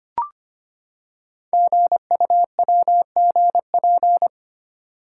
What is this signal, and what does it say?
Morse code, heard as a keyed tone, about 25 wpm, 700 Hz. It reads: GUWGP
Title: Code Groups - 5 Letters - 25wpm